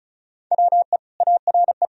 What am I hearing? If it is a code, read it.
Morse code, heard as a keyed tone, about 34 wpm, 700 Hz. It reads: WE ARE
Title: Sentences from Top 1000 Words - Review Part 1 - 35wpm